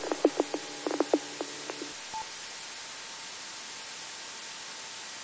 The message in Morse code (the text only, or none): none